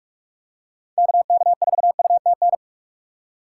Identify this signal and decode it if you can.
Morse code, heard as a keyed tone, about 46 wpm, 700 Hz. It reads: KK4UTN
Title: USA Call Signs - 6 Characters - 3x Character Spacing - 45wpm